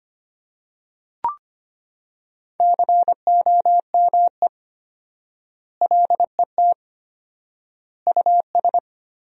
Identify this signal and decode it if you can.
Morse code, heard as a keyed tone, about 25 wpm, 700 Hz. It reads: COME LET US
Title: 4x Word-Spacing - Sentences from Top 300 Words - 25wpm